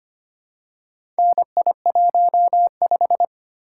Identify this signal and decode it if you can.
Morse code, heard as a keyed tone, about 25 wpm, 690 Hz. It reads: NI15
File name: practice / 25wpm / Four Letter-Number Combinations - 25wpm